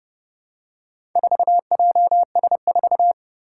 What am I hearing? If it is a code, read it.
Morse code, heard as a keyed tone, about 30 wpm, 700 Hz. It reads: 4JS4
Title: Four Challenging Letter-Number Combinations 30wpm